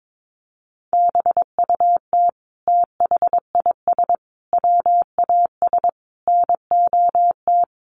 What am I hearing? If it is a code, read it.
Morse code, heard as a keyed tone, about 22 wpm, 700 Hz. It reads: BUT THIS WAS NOT
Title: Sentences from Top 600 Words 22wpm